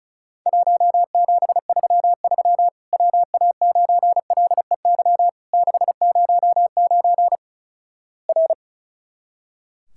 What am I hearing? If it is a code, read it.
Morse code, heard as a keyed tone, about 36 wpm, 700 Hz. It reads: 1733 WA9LEY 609 R
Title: Worked All Europe DX Contest - QTC Exchange - 35wpm